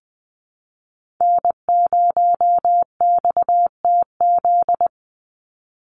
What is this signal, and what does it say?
Morse code, heard as a keyed tone, about 20 wpm, 700 Hz. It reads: N0XTZ
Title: USA Call Signs 20wpm